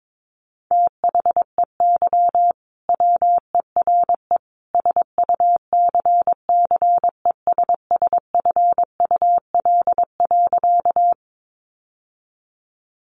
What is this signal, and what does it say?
Morse code, heard as a keyed tone, about 22 wpm, 700 Hz. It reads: THEY WERE SUCCESSFUL.
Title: Sentences from Top 900 Words 22wpm